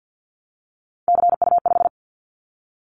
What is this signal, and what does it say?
Morse code, heard as a keyed tone, about 50 wpm, 700 Hz. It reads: CU5
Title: Callsign Prefixes for Top 100 DXCC - 50wpm